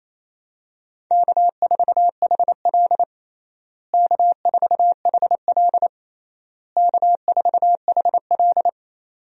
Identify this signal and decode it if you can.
Morse code, heard as a keyed tone, about 28 wpm, 700 Hz. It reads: K4HL K4HL K4HL
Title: USA Call Signs - 4 Characters - Repeated 3x - 28wpm